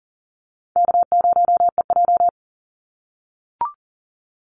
Morse code, heard as a keyed tone, about 40 wpm, 700 Hz. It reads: K0EJ
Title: Easy Call Signs - 40wpm